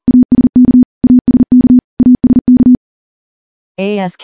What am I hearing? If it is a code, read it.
Morse code, heard as a keyed tone, about 42 wpm, 250 Hz. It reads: ASK ASK ASK
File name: morse / morse-training-40WPM